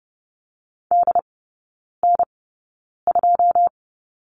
Morse code, heard as a keyed tone, about 30 wpm, spaced as for 9 wpm, 700 Hz. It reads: DN2
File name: Lesson 18e - Letter m - Three Characters 30wpm